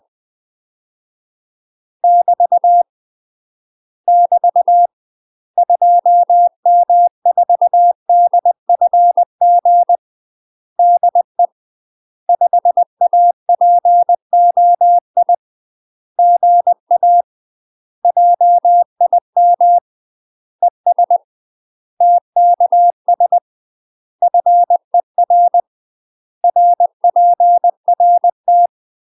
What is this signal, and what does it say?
Morse code, heard as a keyed tone, about 20 wpm, 700 Hz. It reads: <BT> <BT> 2M4DFG DE 5APOI GA JIM ES TKS FER RPRT
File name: UK MASTER QSOs TRAINING / 20WPM / UK MASTER QSOs 0010